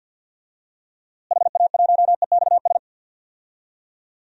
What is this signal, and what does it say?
Morse code, heard as a keyed tone, about 53 wpm, 700 Hz. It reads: HA1EXS
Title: DXCC-100 Call Signs - 6 Characters - Repeated 3x - 50wpm